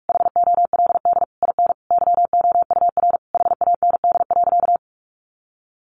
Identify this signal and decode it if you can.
Morse code, heard as a keyed tone, about 44 wpm, 700 Hz. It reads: HOLD IN YOUR HAND.
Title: Sentences from Top 400 Words 45wpm